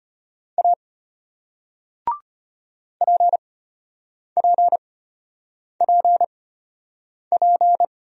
Morse code, heard as a keyed tone, about 32 wpm, spaced as for 8 wpm, 700 Hz. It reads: A PPPP